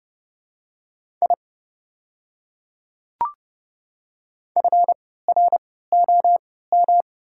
Morse code, heard as a keyed tone, about 31 wpm, spaced as for 17 wpm, 700 Hz. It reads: I FROM
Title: 3x Character Spacing - Top 300 Words - 30wpm